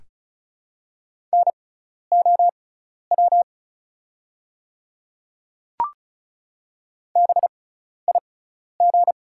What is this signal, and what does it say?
Morse code, heard as a keyed tone, about 34 wpm, spaced as for 12 wpm, 690 Hz. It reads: NOW BIG
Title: 6x Character Spacing - Top 200 Words - 35wpm